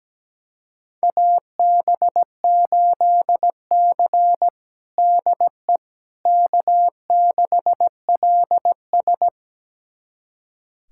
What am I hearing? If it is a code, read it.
Morse code, heard as a keyed tone, about 17 wpm, 700 Hz. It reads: AB8C DE K6LS
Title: USA Callsign DE Callsign - 4 Characters - 17wpm